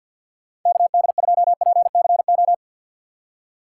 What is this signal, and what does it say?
Morse code, heard as a keyed tone, about 51 wpm, 700 Hz. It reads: KD2PCO
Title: USA Call Signs - 6 Characters - 4x Character Spacing - 50wpm